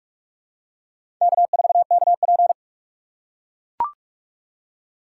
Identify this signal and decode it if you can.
Morse code, heard as a keyed tone, about 44 wpm, 700 Hz. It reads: K4KP
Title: USA Call Signs - 2x Character Spacing - 45wpm